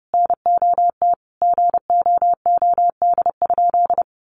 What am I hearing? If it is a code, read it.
Morse code, heard as a keyed tone, about 30 wpm, 700 Hz. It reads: NOT GOOD?
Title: Sentences from Top 1400 Words 30wpm